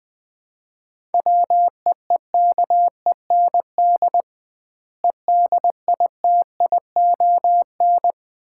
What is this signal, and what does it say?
Morse code, heard as a keyed tone, about 20 wpm, 700 Hz. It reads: WEEKEND EDITION
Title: 2x Word-Spacing - Sets of 5 Words - 20wpm